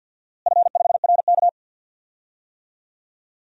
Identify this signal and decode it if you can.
Morse code, heard as a keyed tone, about 52 wpm, 700 Hz. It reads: F5RK
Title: DXCC-100 Call Signs - 4 Characters - Repeated 3x - 50wpm